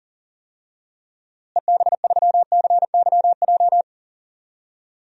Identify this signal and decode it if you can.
Morse code, heard as a keyed tone, about 40 wpm, 700 Hz. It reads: EB3CYJ
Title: DXCC-100 Call Signs - 6 Characters - Repeated 2x - 40wpm